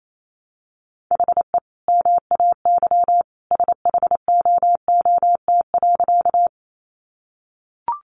Morse code, heard as a keyed tone, about 28 wpm, 700 Hz. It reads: HE MAY SHOOT.
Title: Sentences from Top 1200 Words 28wpm